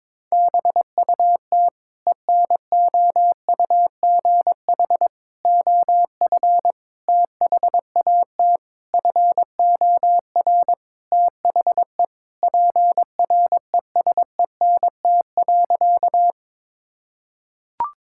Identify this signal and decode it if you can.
Morse code, heard as a keyed tone, about 22 wpm, 700 Hz. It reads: BUT ENOUGH OF THAT FOR THE PRESENT.
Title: Sentences from Top 1000 Words - Review Part 5 - 22wpm